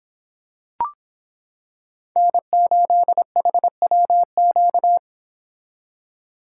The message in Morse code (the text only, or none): N8HWQ